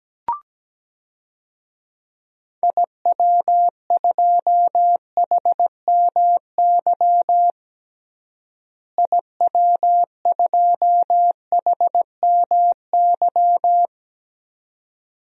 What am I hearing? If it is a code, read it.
Morse code, heard as a keyed tone, about 17 wpm, 700 Hz. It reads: IW2HMY IW2HMY
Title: DXCC-100 Call Signs - 6 Characters - Repeated 2x - 17wpm